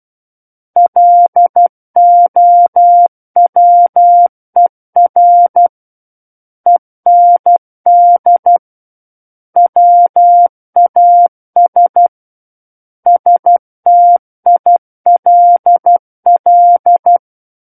Morse code, heard as a keyed tone, about 12 wpm, 700 Hz. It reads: LOWER END WAS STILL
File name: Book One / 03-Chapter 3 - 12 WPM